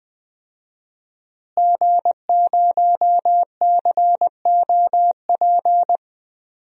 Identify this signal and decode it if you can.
Morse code, heard as a keyed tone, about 20 wpm, 700 Hz. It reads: G0COP
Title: DXCC-100 Call Signs - 5 Characters - Repeated 3x - 20wpm